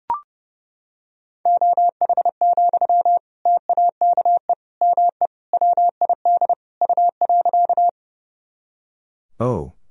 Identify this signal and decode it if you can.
Morse code, heard as a keyed tone, about 32 wpm, 700 Hz. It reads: OH, TAKE ME WID U.
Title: Sentences from Top 500 Words and Abbreviations 30wpm